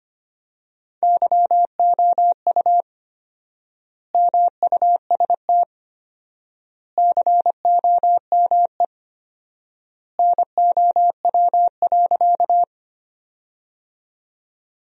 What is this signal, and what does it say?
Morse code, heard as a keyed tone, about 25 wpm, 700 Hz. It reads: YOU MUST COME NOW.
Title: 4x Word-Spacing - Sentences from Top 200 Words - 25wpm